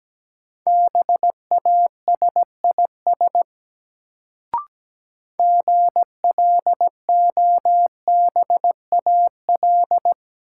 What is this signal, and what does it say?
Morse code, heard as a keyed tone, about 17 wpm, 700 Hz. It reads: BASIS GLOBAL
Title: Sentences from Top 1200 Words - New Word Drill Alt - 17wpm